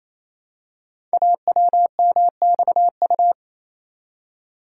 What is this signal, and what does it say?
Morse code, heard as a keyed tone, about 27 wpm, 700 Hz. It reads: AWMXU
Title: Code Groups - 5 Letters - 28wpm